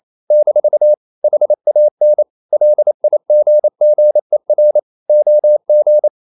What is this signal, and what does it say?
Morse code, heard as a keyed tone, about 28 wpm, 600 Hz. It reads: - HAN LIGGER OG
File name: Ferie QRQ 28wpm